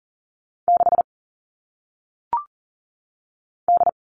Morse code, heard as a keyed tone, about 40 wpm, 700 Hz. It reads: 6 D
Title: All Cut Numbers - 1-Digit 40wpm